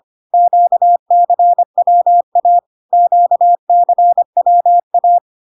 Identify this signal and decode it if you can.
Morse code, heard as a keyed tone, about 25 wpm, 700 Hz. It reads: QCWA QCWA
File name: CW abbreviations alphabetical repeated 3 times 25 25wpm